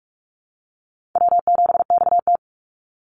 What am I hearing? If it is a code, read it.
Morse code, heard as a keyed tone, about 44 wpm, 700 Hz. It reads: W7XT
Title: Easy Call Signs Repeated 2x - 45wpm